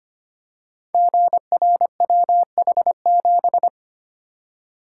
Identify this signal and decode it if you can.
Morse code, heard as a keyed tone, about 25 wpm, 700 Hz. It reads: GRWH7
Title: Code Groups - 5 Letter:Numbers - 25wpm